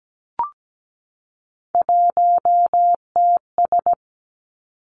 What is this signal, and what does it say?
Morse code, heard as a keyed tone, about 17 wpm, 700 Hz. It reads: 1TS
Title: Three Challenging Letter-Number Combinations 17wpm